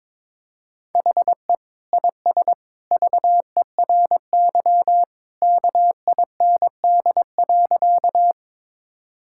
Morse code, heard as a keyed tone, about 21 wpm, 700 Hz. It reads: HE IS VERY KIND.
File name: practice / 22wpm / Sentences with 2-Word Phrases - 22wpm